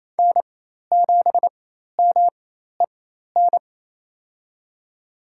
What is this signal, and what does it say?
Morse code, heard as a keyed tone, about 28 wpm, spaced as for 13 wpm, 700 Hz. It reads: N7MEN
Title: USA Call Signs - 5 Characters - 4x Character Spacing - 28wpm